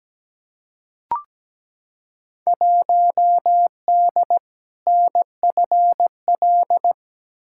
Morse code, heard as a keyed tone, about 17 wpm, 700 Hz. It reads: 1D NFL
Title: Field Day Exchanges - 17wpm